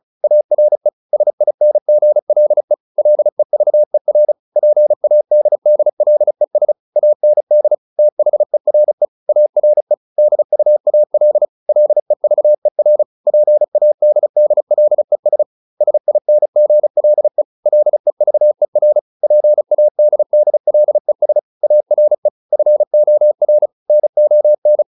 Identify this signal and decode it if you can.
Morse code, heard as a keyed tone, about 34 wpm, 600 Hz. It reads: ARE SINGLE LEVER PADDLES AND THERE ARE DUAL LEVER PADDLES SINGLE LEVER PADDLES ARE FOR NON